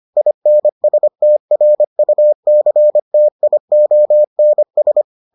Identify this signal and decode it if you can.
Morse code, heard as a keyed tone, about 25 wpm, 600 Hz. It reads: INSTRUCTIONS